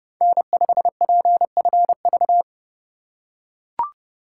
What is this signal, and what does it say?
Morse code, heard as a keyed tone, about 28 wpm, 700 Hz. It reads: N5PFV
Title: USA Call Signs - 5 Characters - 4x Character Spacing - 30wpm